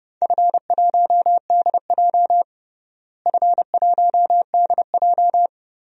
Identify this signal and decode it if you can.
Morse code, heard as a keyed tone, about 28 wpm, 700 Hz. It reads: F1DJ F1DJ
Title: DXCC-100 Call Signs - 4 Characters - Repeated 3x - 30wpm